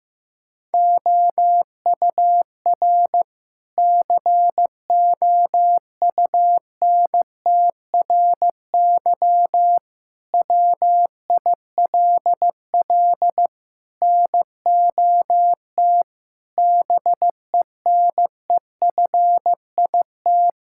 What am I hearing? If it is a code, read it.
Morse code, heard as a keyed tone, about 15 wpm, 700 Hz. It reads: OUR COUNTRY WILL NOT BENEFIT
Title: Sentences from Top 1000 Words - Review Part 6 - 15wpm